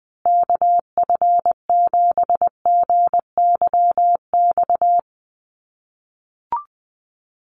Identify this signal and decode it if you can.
Morse code, heard as a keyed tone, about 20 wpm, 700 Hz. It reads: KF7GYX
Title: USA Call Signs Repeated 3x 20wpm